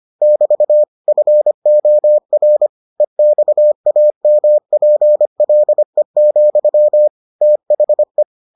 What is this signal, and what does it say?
Morse code, heard as a keyed tone, about 25 wpm, 600 Hz. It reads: <BT> FOR EXAMPLE, THE